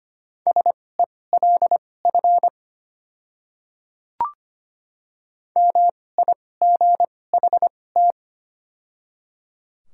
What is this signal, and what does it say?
Morse code, heard as a keyed tone, about 25 wpm, spaced as for 18 wpm, 700 Hz. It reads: SELF MIGHT
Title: 2x Character Spacing - Top 500 Words - 25wpm